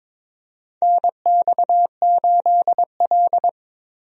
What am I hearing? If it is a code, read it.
Morse code, heard as a keyed tone, about 22 wpm, 700 Hz. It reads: NX8L
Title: USA Call Signs - 4 Characters - Repeated 2x - 22wpm